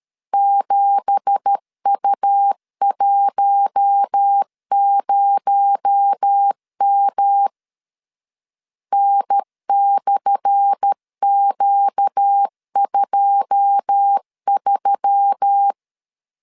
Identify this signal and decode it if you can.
Morse code, heard as a keyed tone, about 13 wpm, 800 Hz. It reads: 7U10M N/Q23
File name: LEZIONI / 10 SIGNS / 10A SLASH 12wpm 2314